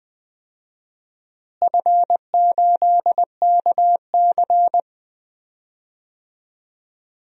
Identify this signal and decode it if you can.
Morse code, heard as a keyed tone, about 20 wpm, 700 Hz. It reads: F8KC
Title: DXCC-100 Call Signs - 4 Characters - Repeated 3x - 20wpm